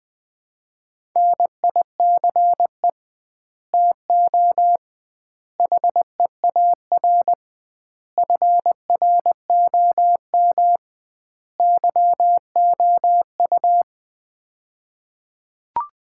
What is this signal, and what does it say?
Morse code, heard as a keyed tone, about 21 wpm, 700 Hz. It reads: NICE TO HEAR FROM YOU
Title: Common QSO Phrases - 2x Word-Spacing - 20wpm